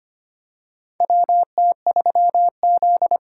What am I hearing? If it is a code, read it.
Morse code, heard as a keyed tone, about 25 wpm, 700 Hz. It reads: WT3Z
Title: USA Call Signs - 4 Characters - 4x Character Spacing - 25wpm